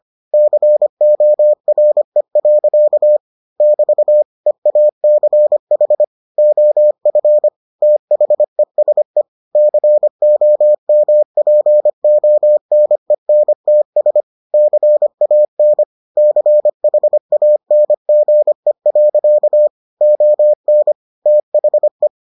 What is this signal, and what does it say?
Morse code, heard as a keyed tone, about 25 wpm, 600 Hz. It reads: CORE. <BT> EACH OF THESE COMPONENTS CAN CHANGE. ON THE